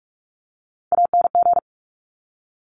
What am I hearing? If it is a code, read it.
Morse code, heard as a keyed tone, about 46 wpm, 700 Hz. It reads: ANG
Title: Top Three Letter Combinations in Words 45wpm